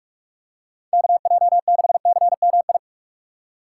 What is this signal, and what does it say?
Morse code, heard as a keyed tone, about 43 wpm, 700 Hz. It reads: KJ6CMI